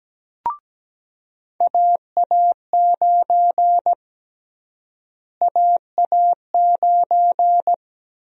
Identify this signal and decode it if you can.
Morse code, heard as a keyed tone, about 17 wpm, 700 Hz. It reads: AA9 AA9
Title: USA Call Sign Prefixes - Repeated 2x - 17wpm